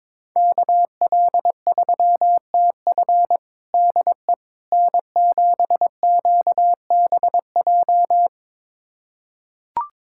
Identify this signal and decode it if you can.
Morse code, heard as a keyed tone, about 21 wpm, 700 Hz. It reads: KL3TF DE N7QBJ
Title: USA Callsign DE Callsign - 5 Characters - 22wpm